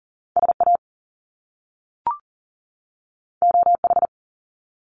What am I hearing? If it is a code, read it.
Morse code, heard as a keyed tone, about 38 wpm, 700 Hz. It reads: SA OH
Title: Top Two Letter Combinations in Words 40wpm